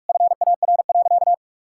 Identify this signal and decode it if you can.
Morse code, heard as a keyed tone, about 47 wpm, 700 Hz. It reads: FAR.
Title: Sentences from Top 1000 Words - Review Part 5 - 45wpm